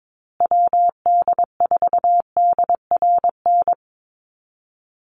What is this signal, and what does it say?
Morse code, heard as a keyed tone, about 22 wpm, 700 Hz. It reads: WD4DRN
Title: USA Call Signs Repeated 2x 22wpm